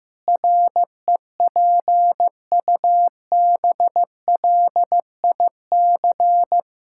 Morse code, heard as a keyed tone, about 15 wpm, 700 Hz. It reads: REPUBLIC
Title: Pangram Challenge v2 - 15wpm